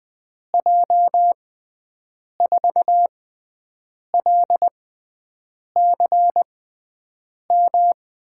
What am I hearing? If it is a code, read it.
Morse code, heard as a keyed tone, about 20 wpm, spaced as for 7 wpm, 700 Hz. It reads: J4LCM
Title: USA Call Signs - 6 Characters - 6x Character Spacing - 20wpm